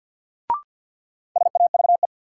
Morse code, heard as a keyed tone, about 50 wpm, 700 Hz. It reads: SAVE
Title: Sentences from Top 1100 Words - New Word Drill - 50wpm